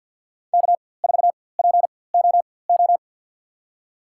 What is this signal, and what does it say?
Morse code, heard as a keyed tone, about 49 wpm, spaced as for 23 wpm, 700 Hz. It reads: K4POO
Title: USA Call Signs - 5 Characters - 4x Character Spacing - 50wpm